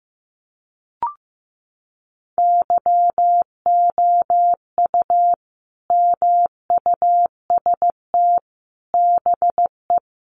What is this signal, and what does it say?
Morse code, heard as a keyed tone, about 15 wpm, 700 Hz. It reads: YOU MUST BE
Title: Sentences from Top 1400 Words 15wpm